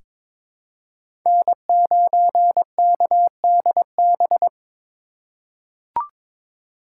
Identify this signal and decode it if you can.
Morse code, heard as a keyed tone, about 22 wpm, 700 Hz. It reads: N9KDB
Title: K1USN SST QSO Practice - 22wpm